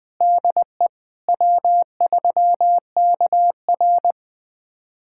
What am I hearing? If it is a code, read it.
Morse code, heard as a keyed tone, about 20 wpm, 700 Hz. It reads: DE W3KR
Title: USA Callsign DE Callsign - 4 Characters - 20wpm